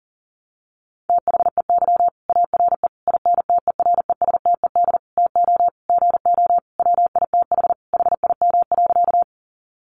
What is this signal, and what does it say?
Morse code, heard as a keyed tone, about 40 wpm, 700 Hz. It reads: THEY ARE INTERESTED TO GO WITH HIM.
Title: Sentences from Top 900 Words 40wpm